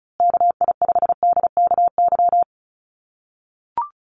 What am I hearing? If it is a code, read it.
Morse code, heard as a keyed tone, about 34 wpm, 700 Hz. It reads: KI5DKY
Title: USA Call Signs Repeated 2x 35wpm